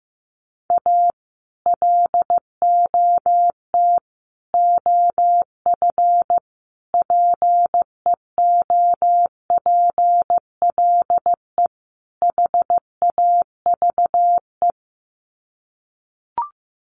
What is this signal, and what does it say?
Morse code, heard as a keyed tone, about 15 wpm, 700 Hz. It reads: A LOT OF PEOPLE HAVE
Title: Sets of 5 Words 15wpm